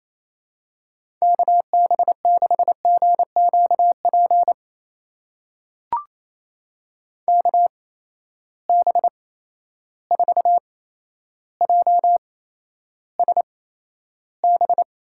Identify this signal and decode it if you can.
Morse code, heard as a keyed tone, about 28 wpm, spaced as for 12 wpm, 700 Hz. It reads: KB6GQP KB4JSB
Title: USA Call Signs - 6 Characters - 8x Character Spacing - 28wpm